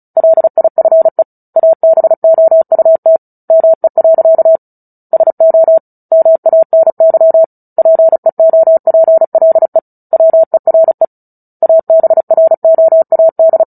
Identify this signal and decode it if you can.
Morse code, heard as a keyed tone, about 35 wpm, 650 Hz. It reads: LIFE ABOUT ME. SO MANY PEOPLE WERE ABROAD
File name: Book Two / 09-Chapter 9 - 35 WPM